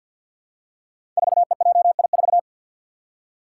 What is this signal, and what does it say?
Morse code, heard as a keyed tone, about 50 wpm, 700 Hz. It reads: 4EJI4
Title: Five Challenging Letter-Number Combinations 50wpm